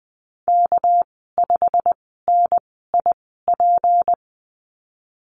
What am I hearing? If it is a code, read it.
Morse code, heard as a keyed tone, about 20 wpm, spaced as for 15 wpm, 700 Hz. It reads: K5NIP
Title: 200 Lesson 32g - Letter k - Call Signs 20wpm